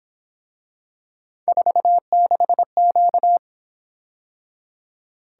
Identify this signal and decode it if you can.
Morse code, heard as a keyed tone, about 27 wpm, 700 Hz. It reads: 46Q
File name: Speed-Racing - Three Letter-Number Combinations - 22wpm